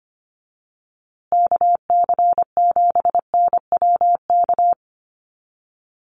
Morse code, heard as a keyed tone, about 25 wpm, 700 Hz. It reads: KC7NWK
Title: USA Call Signs Repeated 3x 25wpm